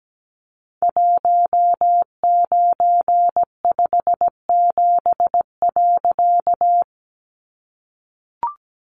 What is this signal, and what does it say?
Morse code, heard as a keyed tone, about 17 wpm, 700 Hz. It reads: 1957.
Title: A Sound Of Thunder 17wpm